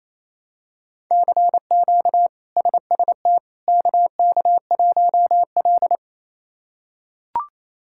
Morse code, heard as a keyed tone, about 28 wpm, 700 Hz. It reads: CQ SST KK1L
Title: K1USN SST QSO Practice - 28wpm